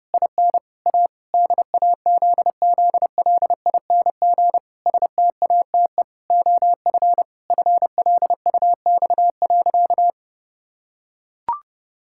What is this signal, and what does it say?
Morse code, heard as a keyed tone, about 30 wpm, 700 Hz. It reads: IN A DAZZLING STATE OF FLUX.
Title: Pangram Challenge v2 - 30wpm